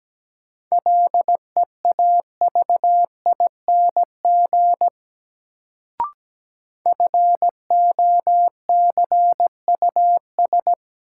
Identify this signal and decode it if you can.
Morse code, heard as a keyed tone, about 17 wpm, 700 Hz. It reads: LEAVING FOCUS
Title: Sentences from Top 1000 Words - Review Part 4 - New Word Drill - 17wpm